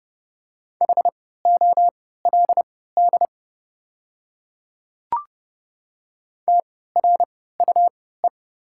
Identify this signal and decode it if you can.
Morse code, heard as a keyed tone, about 31 wpm, spaced as for 17 wpm, 700 Hz. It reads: HOLD TRUE
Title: 3x Character Spacing - Top 500 Words - 30wpm